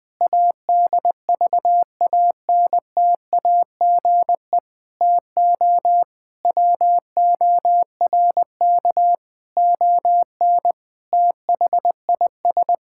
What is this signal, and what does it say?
Morse code, heard as a keyed tone, about 20 wpm, 700 Hz. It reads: ADVANTAGE TO WORK ON THIS
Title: Sentences from Top 1000 Words - Review Part 6 - 20wpm